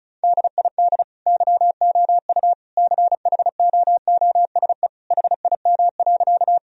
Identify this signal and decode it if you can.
Morse code, heard as a keyed tone, about 34 wpm, 700 Hz. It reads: DID YOU CHOOSE HIM.
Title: Sentences from Top 1000 Words - Review Part 4 - 35wpm